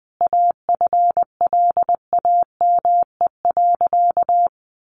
Morse code, heard as a keyed tone, about 20 wpm, 700 Hz. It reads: AFLAME.
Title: A Sound Of Thunder 20wpm